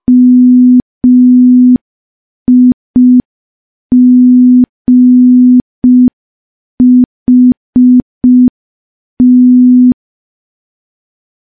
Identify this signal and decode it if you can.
Morse code, heard as a keyed tone, about 5 wpm, 250 Hz. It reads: MIGHT
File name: morse / morse-training-05WPM